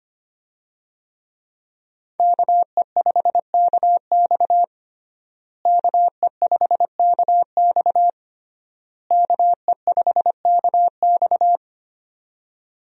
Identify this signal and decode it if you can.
Morse code, heard as a keyed tone, about 25 wpm, 700 Hz. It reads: KE5KX KE5KX KE5KX
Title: USA Call Signs - 5 Characters - Repeated 3x - 25wpm